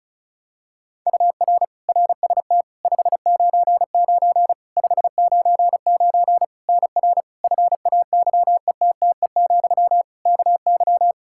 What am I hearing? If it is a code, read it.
Morse code, heard as a keyed tone, about 36 wpm, 700 Hz. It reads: UR RST 599 599 NR FAYETTE, KY
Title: Practice QSO - Exchange 1 - Beginning - 35wpm